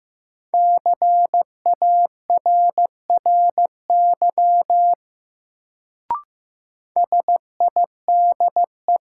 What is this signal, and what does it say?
Morse code, heard as a keyed tone, about 15 wpm, 700 Hz. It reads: CARRY SIDE
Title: Sentences from Top 400 Words - New Word Drill - 15wpm